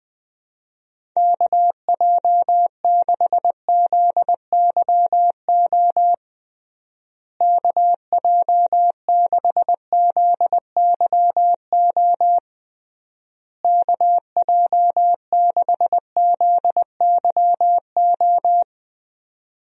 Morse code, heard as a keyed tone, about 20 wpm, 700 Hz. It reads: KJ6ZYO KJ6ZYO KJ6ZYO
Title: USA Call Signs - 6 Characters - Repeated 3x - 20wpm